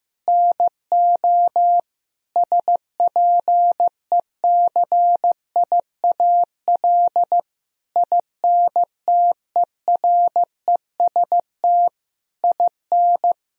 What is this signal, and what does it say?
Morse code, heard as a keyed tone, about 15 wpm, 700 Hz. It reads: NO SPECIAL INTEREST IN
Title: Sentences from Top 1000 Words - Review Part 3 - 15wpm